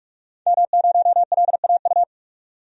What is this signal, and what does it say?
Morse code, heard as a keyed tone, about 45 wpm, 700 Hz. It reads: M0LAU
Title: DXCC-100 Call Signs - 5 Characters - 45wpm